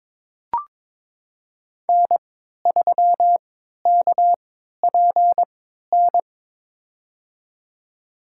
Morse code, heard as a keyed tone, about 22 wpm, spaced as for 13 wpm, 700 Hz. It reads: N3KPN
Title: USA Call Signs - 5 Characters - 3x Character Spacing - 22wpm